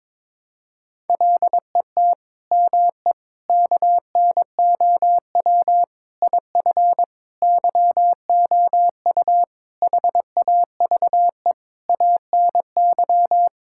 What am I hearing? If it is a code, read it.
Morse code, heard as a keyed tone, about 21 wpm, 700 Hz. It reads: LET ME KNOW IF YOU HAVE ANY
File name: practice / 22wpm / Sets of 8 Words - 22wpm